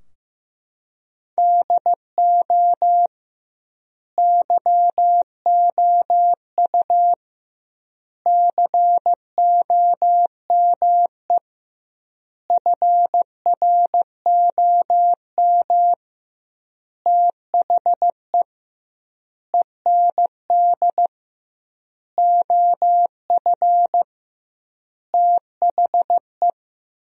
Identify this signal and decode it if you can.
Morse code, heard as a keyed tone, about 15 wpm, 700 Hz. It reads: DO YOU COME FROM THE END OF THE
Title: 2x Word-Spacing - Sentences from Top 200 Words - 15wpm